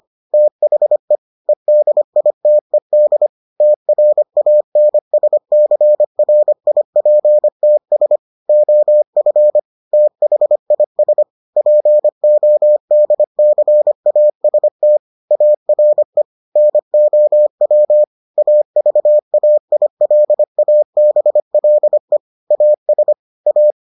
Morse code, heard as a keyed tone, about 25 wpm, 600 Hz. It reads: THE EDITED TRANSCRIPTS OF THIS PODCAST ARE NOW AVAILABLE AS A